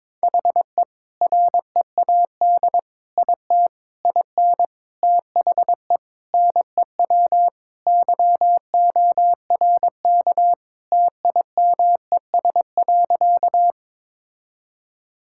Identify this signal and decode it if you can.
Morse code, heard as a keyed tone, about 22 wpm, 700 Hz. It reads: HE READ IT IN THE NEW YORK TIMES.
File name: practice / 22wpm / Sentences with 4-Word Phrases - 22wpm